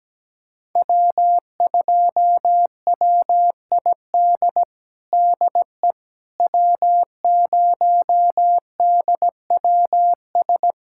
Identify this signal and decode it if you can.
Morse code, heard as a keyed tone, about 17 wpm, 700 Hz. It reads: W2WID DE W0DWS
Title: USA Callsign DE Callsign - 5 Characters - 17wpm